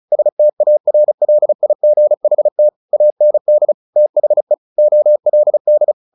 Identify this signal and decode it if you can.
Morse code, heard as a keyed tone, about 34 wpm, 600 Hz. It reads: STARLIGHT AND THE OLD
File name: the old man and the sea 2 35wpm